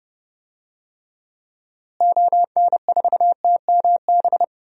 Morse code, heard as a keyed tone, about 30 wpm, 700 Hz. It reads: ON4TMB
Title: DXCC-100 Call Signs - 6 Characters - Repeated 2x - 30wpm